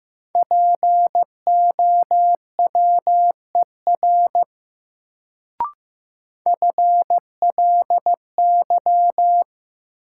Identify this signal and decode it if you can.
Morse code, heard as a keyed tone, about 15 wpm, 700 Hz. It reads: POWER FLY
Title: Sentences from Top 500 Words - New Word Drill - 15wpm